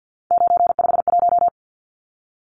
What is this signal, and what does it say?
Morse code, heard as a keyed tone, about 50 wpm, 700 Hz. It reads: 951
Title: Numbers - 3-Digit 50wpm